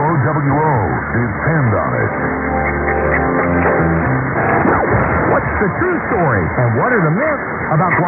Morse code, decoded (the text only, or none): none